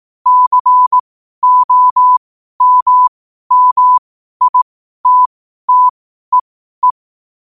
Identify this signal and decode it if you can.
Morse code, heard as a keyed tone, about 18 wpm, spaced as for 12 wpm, 1000 Hz. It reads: COMMITTEE
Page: New quotes every day in morse code at 12 Words per minute.